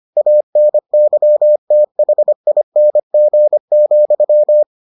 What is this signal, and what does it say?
Morse code, heard as a keyed tone, about 25 wpm, 600 Hz. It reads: ANYTHING,